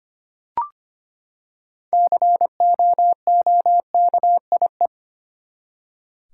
Morse code, heard as a keyed tone, about 25 wpm, 700 Hz. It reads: COOKIE
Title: Six Letter Words - 25wpm